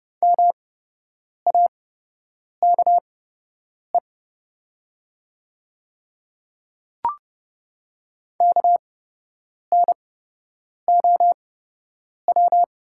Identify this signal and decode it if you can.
Morse code, heard as a keyed tone, about 30 wpm, spaced as for 8 wpm, 700 Hz. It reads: MAKE KNOW
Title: 8x Character Spacing - Top 100 Words - 30wpm